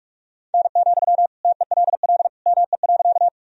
Morse code, heard as a keyed tone, about 46 wpm, 700 Hz. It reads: N, TELL ME.
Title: Sentences from Top 200 Words and Abbreviations 45wpm